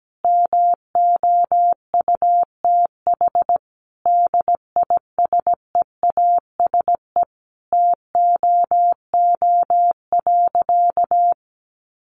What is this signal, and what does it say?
Morse code, heard as a keyed tone, about 17 wpm, 700 Hz. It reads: MOUTH DISEASE TOO.
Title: Sentences from Top 1500 Words 17wpm